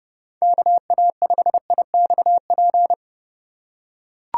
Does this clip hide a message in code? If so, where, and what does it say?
Yes: Morse code from 0.4 to 2.9 s, heard as a keyed tone, about 30 wpm, 700 Hz. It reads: KA5IXP